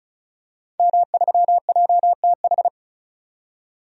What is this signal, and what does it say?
Morse code, heard as a keyed tone, about 36 wpm, 700 Hz. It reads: M3JTH
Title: DXCC-100 Call Signs - 5 Characters - 35wpm